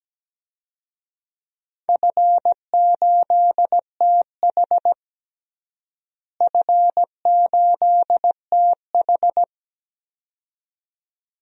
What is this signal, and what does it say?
Morse code, heard as a keyed tone, about 17 wpm, 700 Hz. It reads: F8TH F8TH
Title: DXCC-100 Call Signs - 4 Characters - Repeated 2x - 17wpm